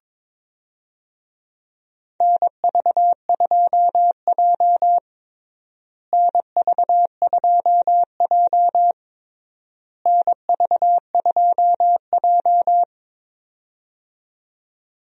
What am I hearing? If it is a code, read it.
Morse code, heard as a keyed tone, about 22 wpm, 700 Hz. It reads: NV2J NV2J NV2J
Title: USA Call Signs - 4 Characters - Repeated 3x - 22wpm